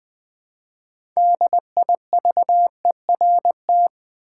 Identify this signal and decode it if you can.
Morse code, heard as a keyed tone, about 20 wpm, 700 Hz. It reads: DIVERT
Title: Build A Word - 3 Letters At A Time - 6-Letter Words - 20wpm